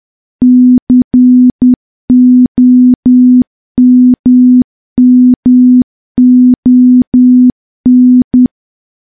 Morse code, heard as a keyed tone, about 10 wpm, 250 Hz. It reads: COMMON
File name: morse / morse-training-10WPM